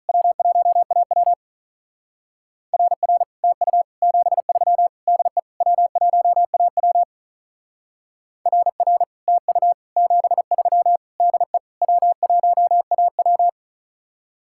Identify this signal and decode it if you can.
Morse code, heard as a keyed tone, about 43 wpm, 700 Hz. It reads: W1AW RR TU 73 DE W1AW RR TU 73 DE W1AW
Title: Speed-Racing Common QSO Phrases - 35wpm